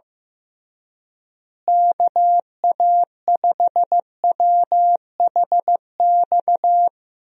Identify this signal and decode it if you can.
Morse code, heard as a keyed tone, about 15 wpm, 700 Hz. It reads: KA5WHX
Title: USA Call Signs - 6 Characters - Repeated 3x - 15wpm